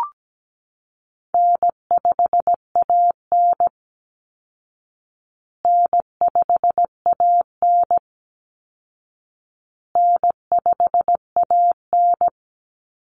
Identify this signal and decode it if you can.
Morse code, heard as a keyed tone, about 17 wpm, 700 Hz. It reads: N5AN N5AN N5AN
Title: Call Signs Repeated 2x - 17wpm